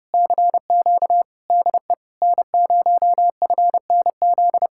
Morse code, heard as a keyed tone, about 33 wpm, 700 Hz. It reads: CQ DE N0FNZ